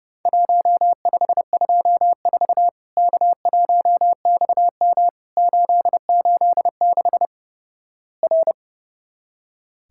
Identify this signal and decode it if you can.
Morse code, heard as a keyed tone, about 31 wpm, 700 Hz. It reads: 1524 K1XM 886 R
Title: Worked All Europe DX Contest - QTC Exchange - 30wpm